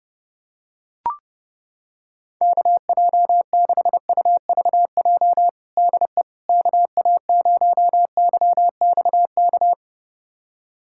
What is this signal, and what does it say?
Morse code, heard as a keyed tone, about 30 wpm, 700 Hz. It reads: KJ6UVJ DE KA0YXK
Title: USA Callsign DE Callsign - 6 Characters - 30wpm